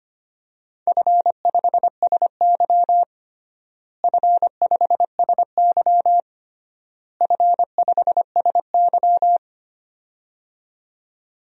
Morse code, heard as a keyed tone, about 25 wpm, 700 Hz. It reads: F5SY F5SY F5SY
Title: DXCC-100 Call Signs - 4 Characters - Repeated 3x - 25wpm